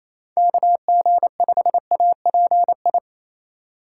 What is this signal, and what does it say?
Morse code, heard as a keyed tone, about 36 wpm, 700 Hz. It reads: KG5API